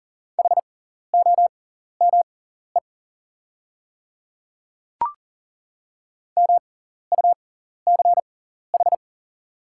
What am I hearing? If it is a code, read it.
Morse code, heard as a keyed tone, about 38 wpm, spaced as for 14 wpm, 690 Hz. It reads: HOME MUCH
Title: 6x Character Spacing - Top 200 Words - 40wpm